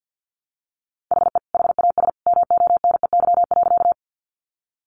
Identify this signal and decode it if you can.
Morse code, heard as a keyed tone, about 50 wpm, 700 Hz. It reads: HE HAS MONEY.
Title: Sentences from Top 400 Words 50wpm